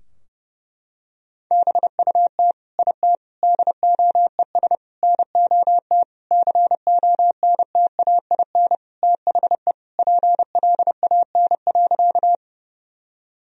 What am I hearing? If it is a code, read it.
Morse code, heard as a keyed tone, about 30 wpm, 700 Hz. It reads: BUT IT DOES NOT CONTAIN THE PLAN.
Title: Sentences from Top 1000 Words - Review Part 2 - 30wpm